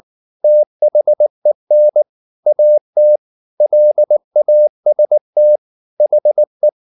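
Morse code, heard as a keyed tone, about 19 wpm, 600 Hz. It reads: THEN AT LAST HE
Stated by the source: Morse code at 19 wpm from CW med Gnister, reading Ken Follett's novel Never in English.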